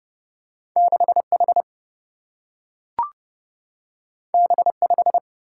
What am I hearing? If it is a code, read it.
Morse code, heard as a keyed tone, about 30 wpm, 700 Hz. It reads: 6H B5
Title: Single-Letter Single-Number 30wpm